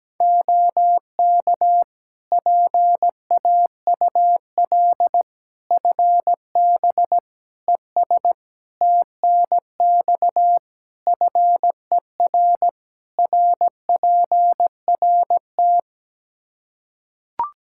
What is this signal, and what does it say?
Morse code, heard as a keyed tone, about 17 wpm, 700 Hz. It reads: OK PAUL FB ES TNX FER RPRT
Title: Practice QSO - Exchange 2 - Info - 17wpm